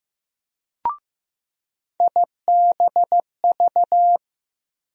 Morse code, heard as a keyed tone, about 15 wpm, 700 Hz. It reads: IBV
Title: USA Call Sign Postfixes - 3 Letters 15wpm